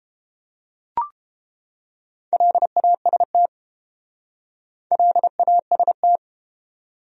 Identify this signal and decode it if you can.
Morse code, heard as a keyed tone, about 32 wpm, 700 Hz. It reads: LAST LAST